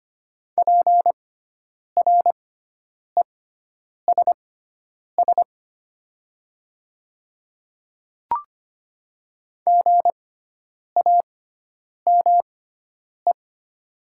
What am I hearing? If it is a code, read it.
Morse code, heard as a keyed tone, about 25 wpm, spaced as for 9 wpm, 700 Hz. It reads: PRESS GAME
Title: 6x Character Spacing - Top 500 Words - 25wpm